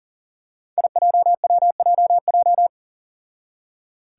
Morse code, heard as a keyed tone, about 40 wpm, 700 Hz. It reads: IJWJJ
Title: Five Challenging Letter-Number Combinations 40wpm